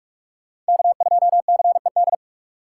Morse code, heard as a keyed tone, about 59 wpm, spaced as for 52 wpm, 700 Hz. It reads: K2CED
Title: Speed Racing - USA Call Signs 45wpm